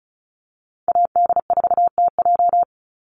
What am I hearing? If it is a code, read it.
Morse code, heard as a keyed tone, about 34 wpm, 700 Hz. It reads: AD4TJ
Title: Call Signs Repeated 3x - 35wpm